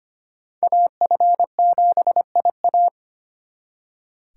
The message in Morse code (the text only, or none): AF7IA